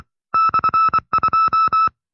Morse code, heard as a keyed tone, about 24 wpm, 1310 Hz. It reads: /2
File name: FreeFiles / CW-practice / 20WPM-side2